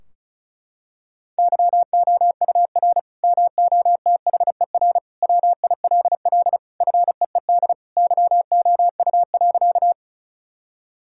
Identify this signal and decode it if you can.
Morse code, heard as a keyed tone, about 36 wpm, 700 Hz. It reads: YOUR MOTHER WILL FEED YOU.
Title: Sentences from Top 1600 Words 35wpm